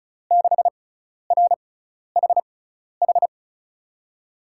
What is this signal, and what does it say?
Morse code, heard as a keyed tone, about 36 wpm, spaced as for 12 wpm, 700 Hz. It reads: 6RHH
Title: USA Call Signs - 6 Characters - 6x Character Spacing - 35wpm